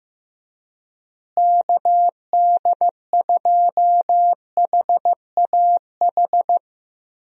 Morse code, heard as a keyed tone, about 15 wpm, 700 Hz. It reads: KD2HAH
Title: USA Call Signs - 6 Characters - Repeated 3x - 15wpm